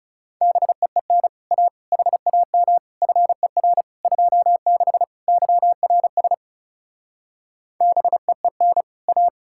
Morse code, heard as a keyed tone, about 36 wpm, 700 Hz. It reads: BEEN A HAM FER 26 YRS BEEN A
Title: Speed-Racing Common QSO Phrases - 30wpm